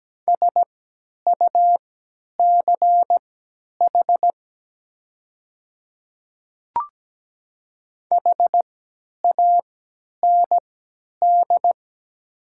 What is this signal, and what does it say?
Morse code, heard as a keyed tone, about 17 wpm, spaced as for 10 wpm, 700 Hz. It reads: SUCH HAND
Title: 3x Character Spacing - Top 200 Words - 17wpm